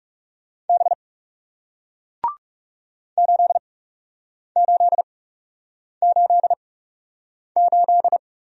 Morse code, heard as a keyed tone, about 39 wpm, spaced as for 8 wpm, 700 Hz. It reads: B 8888